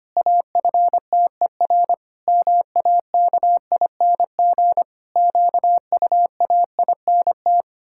Morse code, heard as a keyed tone, about 25 wpm, 700 Hz. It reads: AFTER MAKING QUAINT
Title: Pangram Challenge v2 - 25wpm